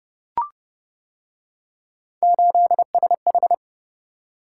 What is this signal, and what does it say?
Morse code, heard as a keyed tone, about 29 wpm, 700 Hz. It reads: 8SH